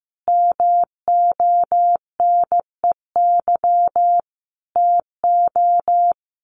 Morse code, heard as a keyed tone, about 15 wpm, 700 Hz. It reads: MONEY TO
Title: Sets of 5 Words Encore2 15wpm